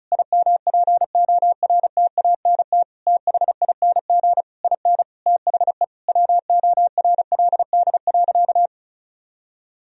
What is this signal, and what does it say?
Morse code, heard as a keyed tone, about 34 wpm, 700 Hz. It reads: IMPORTANT THING IN THE WORLD.
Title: Sentences from Top 1000 Words - Review Part 1 - 35wpm